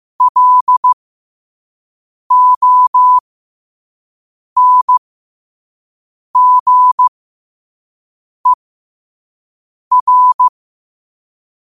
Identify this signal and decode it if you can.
Morse code, heard as a keyed tone, about 15 wpm, spaced as for 5 wpm, 1000 Hz. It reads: LONGER
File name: qotdfiles / 5wpm